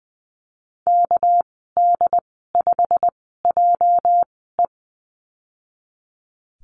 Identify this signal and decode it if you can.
Morse code, heard as a keyed tone, about 20 wpm, spaced as for 14 wpm, 700 Hz. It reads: KD5JE
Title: 231 Lesson 37g - Letter x - Call Signs 20wpm